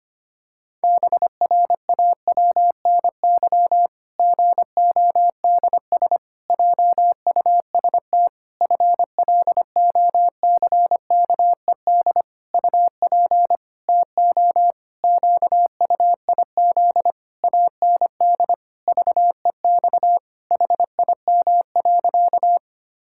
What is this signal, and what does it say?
Morse code, heard as a keyed tone, about 25 wpm, 700 Hz. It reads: BRAWNY GODS JUST FLOCKED UP TO QUIZ AND VEX HIM.
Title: Pangram Challenge v2 - 25wpm